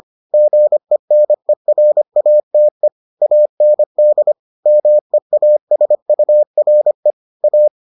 Morse code, heard as a keyed tone, about 25 wpm, 600 Hz. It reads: GENERATE AND MEASURE A